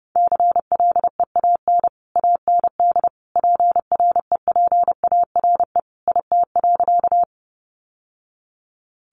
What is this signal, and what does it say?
Morse code, heard as a keyed tone, about 30 wpm, 700 Hz. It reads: CLEAN AND PREPARE IT.
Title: Sentences from Top 1100 Words 30wpm